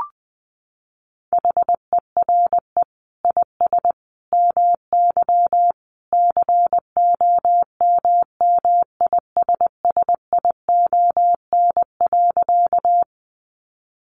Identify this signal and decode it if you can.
Morse code, heard as a keyed tone, about 20 wpm, 700 Hz. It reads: HERE IS MY COMMISSION.
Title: Sentences from Top 1400 Words 20wpm